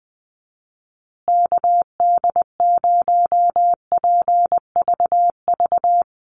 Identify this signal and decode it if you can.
Morse code, heard as a keyed tone, about 20 wpm, 700 Hz. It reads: KD0PVV
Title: USA Call Signs Repeated 3x 20wpm